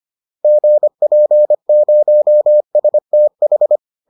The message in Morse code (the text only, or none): GP0STH